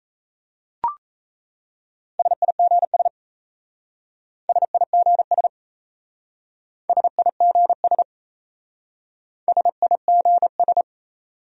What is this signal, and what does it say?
Morse code, heard as a keyed tone, about 37 wpm, 700 Hz. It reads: SIGS SIGS SIGS SIGS